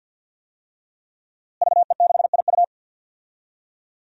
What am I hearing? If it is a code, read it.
Morse code, heard as a keyed tone, about 52 wpm, 700 Hz. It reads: VE6IU
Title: DXCC-100 Call Signs - 5 Characters - Repeated 3x - 50wpm